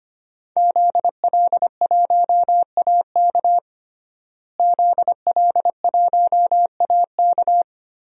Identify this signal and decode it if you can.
Morse code, heard as a keyed tone, about 25 wpm, 700 Hz. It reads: ZL1AK ZL1AK
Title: DXCC-100 Call Signs - 5 Characters - Repeated 3x - 25wpm